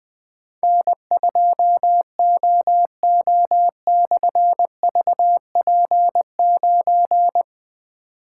Morse code, heard as a keyed tone, about 20 wpm, 700 Hz. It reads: N2OO/VP9
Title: Difficult Call Signs 20wpm